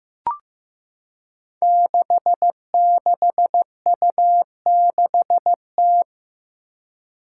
Morse code, heard as a keyed tone, about 15 wpm, 700 Hz. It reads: 66U6T
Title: Five Challenging Letter-Number Combinations 15wpm